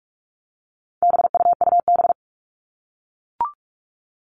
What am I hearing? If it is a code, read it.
Morse code, heard as a keyed tone, about 44 wpm, 700 Hz. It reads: BUUB
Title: All Cut Numbers - 4-Digit 45wpm